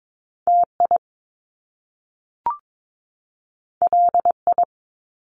Morse code, heard as a keyed tone, about 22 wpm, 700 Hz. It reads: TI LI
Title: Top Two Letter Combinations in Words 22wpm